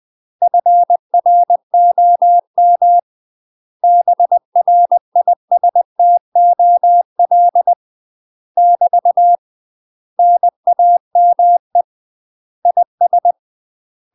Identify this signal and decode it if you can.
Morse code, heard as a keyed tone, about 20 wpm, 700 Hz. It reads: FROM BRISTOL <BT> NAME IS
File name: UK MASTER QSOs TRAINING / 20WPM / UK MASTER QSOs 0023